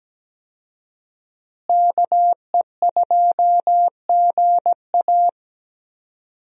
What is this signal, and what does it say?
Morse code, heard as a keyed tone, about 17 wpm, 700 Hz. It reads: KE2GA
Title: USA Call Signs - 5 Characters - Repeated 3x - 17wpm